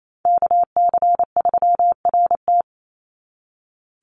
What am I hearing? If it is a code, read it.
Morse code, heard as a keyed tone, about 28 wpm, 700 Hz. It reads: KC3RT
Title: NA SOTA Chaser Callsigns 28wpm